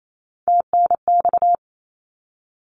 Morse code, heard as a keyed tone, about 28 wpm, 700 Hz. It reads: TNX
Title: Abbreviations 28wpm